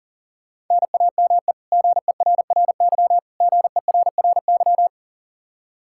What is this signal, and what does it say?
Morse code, heard as a keyed tone, about 40 wpm, 700 Hz. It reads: NAME GERRY GERRY
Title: Practice QSO - Exchange 1 - Beginning - 40wpm